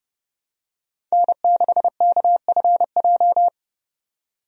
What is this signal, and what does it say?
Morse code, heard as a keyed tone, about 31 wpm, 700 Hz. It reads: N6KFJ
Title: USA Call Signs - 5 Characters - 30wpm